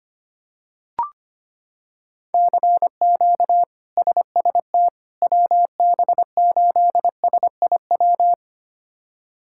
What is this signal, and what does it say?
Morse code, heard as a keyed tone, about 25 wpm, 700 Hz. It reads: CQ SST WB8SIW
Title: K1USN SST QSO Practice - 25wpm